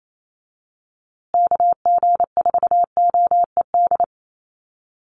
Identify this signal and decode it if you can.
Morse code, heard as a keyed tone, about 28 wpm, 700 Hz. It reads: KG4OED
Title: USA Call Signs Repeated 3x 28wpm